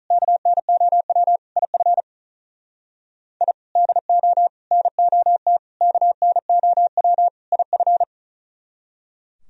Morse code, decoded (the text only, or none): KNOW IF I DO NOT KNOW IF